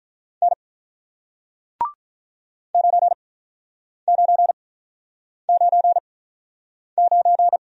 Morse code, heard as a keyed tone, about 48 wpm, spaced as for 9 wpm, 700 Hz. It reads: N 9999